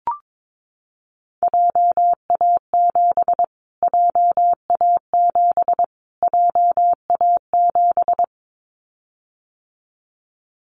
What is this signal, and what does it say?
Morse code, heard as a keyed tone, about 22 wpm, 700 Hz. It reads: JA7 JA7 JA7
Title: Callsign Prefixes for Top 25 DXCC - Repeated 3x - 22wpm